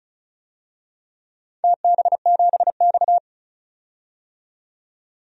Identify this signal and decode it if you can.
Morse code, heard as a keyed tone, about 36 wpm, 700 Hz. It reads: TB7X
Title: DXCC-100 Call Signs - 4 Characters - 35wpm